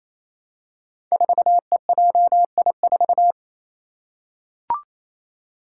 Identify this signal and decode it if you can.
Morse code, heard as a keyed tone, about 28 wpm, 700 Hz. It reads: 4EJI4
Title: Five Challenging Letter-Number Combinations 28wpm